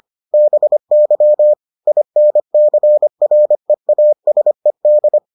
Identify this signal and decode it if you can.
Morse code, heard as a keyed tone, about 25 wpm, 600 Hz. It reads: BY INCREASED